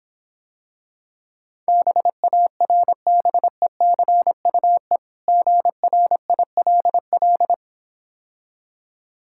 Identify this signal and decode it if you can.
Morse code, heard as a keyed tone, about 26 wpm, 700 Hz. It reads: BARBECUE GRILL